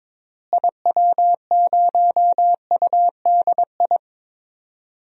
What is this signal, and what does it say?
Morse code, heard as a keyed tone, about 21 wpm, 700 Hz. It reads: IW0UDI
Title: DXCC-100 Call Signs - 6 Characters - Repeated 2x - 22wpm